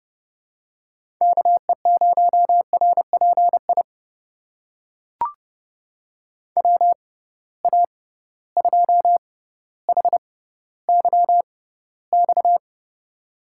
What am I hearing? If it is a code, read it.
Morse code, heard as a keyed tone, about 29 wpm, spaced as for 15 wpm, 700 Hz. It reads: KE0RPI WA2HYX
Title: USA Call Signs - 6 Characters - 6x Character Spacing - 30wpm